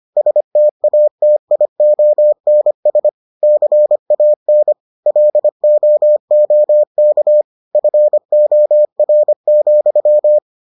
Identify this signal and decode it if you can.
Morse code, heard as a keyed tone, about 25 wpm, 600 Hz. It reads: STATIONS CAN LOOK FOR,